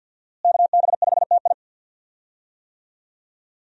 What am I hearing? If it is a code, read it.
Morse code, heard as a keyed tone, about 51 wpm, 700 Hz. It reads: KB5TI
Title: USA Call Signs - 5 Characters - Repeated 2x - 50wpm